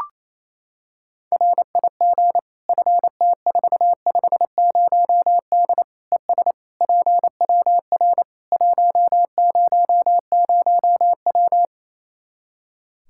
Morse code, heard as a keyed tone, about 28 wpm, 700 Hz. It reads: RIG FT450D ES PWR 100W
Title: K1USN SST QSO Practice 28wpm